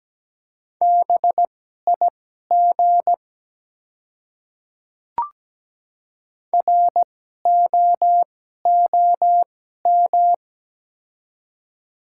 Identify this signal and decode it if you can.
Morse code, heard as a keyed tone, about 17 wpm, spaced as for 12 wpm, 690 Hz. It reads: BIG ROOM
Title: 2x Character Spacing - Top 300 Words - 17wpm